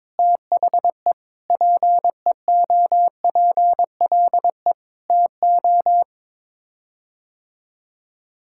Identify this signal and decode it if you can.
Morse code, heard as a keyed tone, about 21 wpm, 700 Hz. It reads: THE PEOPLE TO
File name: practice / 22wpm / Sets of 6 Words - 22wpm